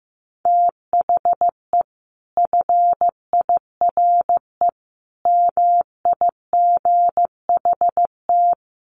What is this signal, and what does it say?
Morse code, heard as a keyed tone, about 15 wpm, 700 Hz. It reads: THE FIRE MIGHT
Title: Sentences from Top 1300 Words 15wpm